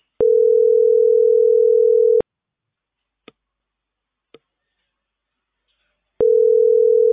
Ringback tone (North America), 440 + 480 Hz.